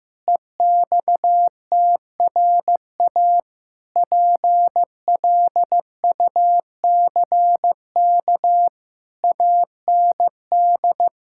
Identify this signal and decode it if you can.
Morse code, heard as a keyed tone, about 15 wpm, 700 Hz. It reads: EXTRA PLUCK AND
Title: Pangram Challenge v2 - 15wpm